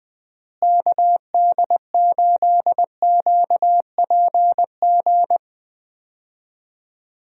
Morse code, heard as a keyed tone, about 20 wpm, 700 Hz. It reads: KD8QPG
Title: USA Call Signs - 6 Characters - Repeated 3x - 20wpm